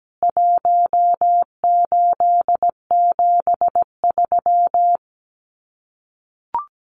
Morse code, heard as a keyed tone, about 17 wpm, 700 Hz. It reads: 1873
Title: CWT Exchange 17wpm